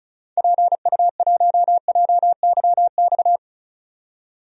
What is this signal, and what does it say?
Morse code, heard as a keyed tone, about 36 wpm, 700 Hz. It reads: PU1JYX
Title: DXCC-100 Call Signs - 6 Characters - 35wpm